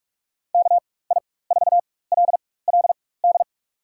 Morse code, heard as a keyed tone, about 46 wpm, spaced as for 21 wpm, 700 Hz. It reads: KI4LLD
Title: USA Call Signs - 4x Character Spacing - 45wpm